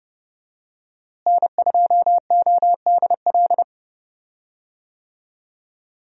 Morse code, heard as a keyed tone, about 30 wpm, 700 Hz. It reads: N2ODL
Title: USA Call Signs - 5 Characters - Repeated 2x - 30wpm